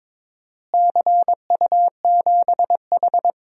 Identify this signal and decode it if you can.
Morse code, heard as a keyed tone, about 22 wpm, 700 Hz. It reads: CU7H
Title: DXCC-100 Call Signs - 4 Characters - Repeated 3x - 22wpm